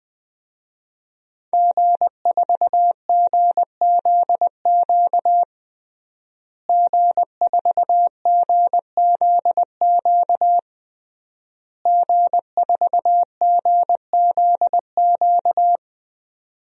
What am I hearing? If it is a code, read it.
Morse code, heard as a keyed tone, about 20 wpm, 700 Hz. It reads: G4GZQ G4GZQ G4GZQ
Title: DXCC-100 Call Signs - 5 Characters - Repeated 3x - 20wpm